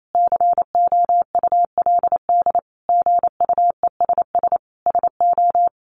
Morse code, heard as a keyed tone, about 28 wpm, 700 Hz. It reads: COULD GUESS SO
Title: Sentences from Top 1500 Words 28wpm